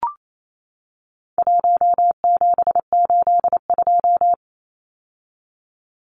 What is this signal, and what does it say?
Morse code, heard as a keyed tone, about 28 wpm, 700 Hz. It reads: 1782
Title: Numbers - 4-Digit 28wpm